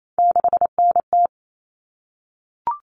Morse code, heard as a keyed tone, about 28 wpm, 700 Hz. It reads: 6NT
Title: All Cut Numbers - 3-Digit 28wpm